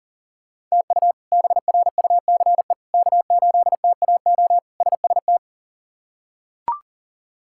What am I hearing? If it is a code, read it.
Morse code, heard as a keyed tone, about 40 wpm, 700 Hz. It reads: TU BRUCE K8TAO SST
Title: K1USN SST QSO Practice - 40wpm